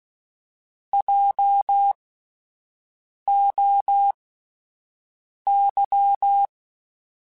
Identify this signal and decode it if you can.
Morse code, heard as a keyed tone, about 16 wpm, spaced as for 5 wpm, 800 Hz. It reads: JOY
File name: Spaced Alphas / Spaced Alphas mp3s / Js sp @10